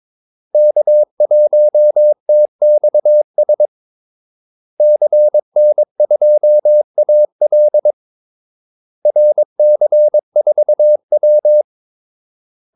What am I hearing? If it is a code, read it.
Morse code, heard as a keyed tone, about 22 wpm, 600 Hz. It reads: K1TXS CN2AL RC4W
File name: Callsigns-22wpm